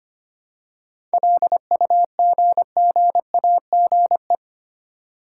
Morse code, heard as a keyed tone, about 25 wpm, 700 Hz. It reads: LUGGAGE
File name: practice / 25wpm / Easy Words with Hints - 25wpm